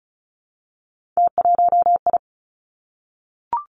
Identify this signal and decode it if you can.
Morse code, heard as a keyed tone, about 36 wpm, 700 Hz. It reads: T1I
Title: Three Challenging Letter-Number Combinations 35wpm